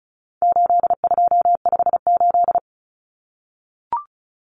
Morse code, heard as a keyed tone, about 34 wpm, 700 Hz. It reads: 8258
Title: Common Cut Numbers - 4-Digit 35wpm